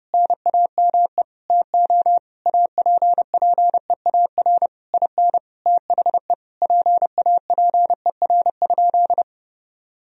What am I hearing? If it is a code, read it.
Morse code, heard as a keyed tone, about 30 wpm, 700 Hz. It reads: NAME TO APPEAR IN THE PAPER?
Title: Sentences from Top 1000 Words - Review Part 2 - 30wpm